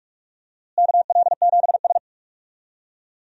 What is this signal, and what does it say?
Morse code, heard as a keyed tone, about 46 wpm, 700 Hz. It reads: KL7S
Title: DXCC-100 Call Signs - 4 Characters - Repeated 3x - 45wpm